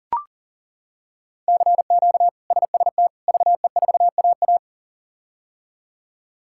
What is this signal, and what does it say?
Morse code, heard as a keyed tone, about 40 wpm, 700 Hz. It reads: CQ SST VE4AA
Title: K1USN SST QSO Practice - 40wpm